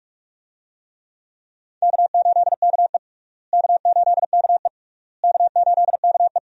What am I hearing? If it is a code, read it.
Morse code, heard as a keyed tone, about 46 wpm, 700 Hz. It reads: K8KE K8KE K8KE
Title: USA Call Signs - 4 Characters - Repeated 3x - 45wpm